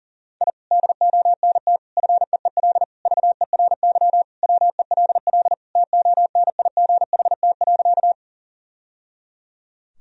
Morse code, heard as a keyed tone, about 40 wpm, 700 Hz. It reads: I DONT FEEL VERY WELL TONIGHT.
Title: Sentences from Top 1000 Words - Review Part 3 - 40wpm